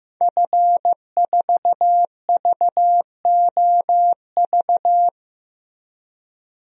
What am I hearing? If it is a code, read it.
Morse code, heard as a keyed tone, about 15 wpm, 700 Hz. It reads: F4VOV
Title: DXCC-100 Call Signs - 5 Characters - 15wpm